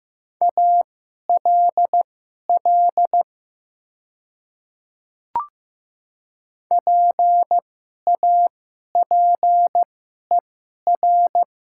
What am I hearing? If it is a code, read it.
Morse code, heard as a keyed tone, about 15 wpm, spaced as for 11 wpm, 700 Hz. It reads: ALL PAPER
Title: 2x Character Spacing - Top 500 Words - 15wpm